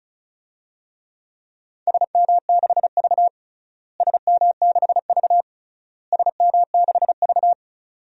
Morse code, heard as a keyed tone, about 34 wpm, 700 Hz. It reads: SM6V SM6V SM6V
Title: DXCC-100 Call Signs - 4 Characters - Repeated 3x - 35wpm